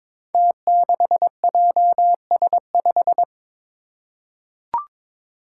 Morse code, heard as a keyed tone, about 22 wpm, 700 Hz. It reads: T6JS5
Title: Five Challenging Letter-Number Combinations 22wpm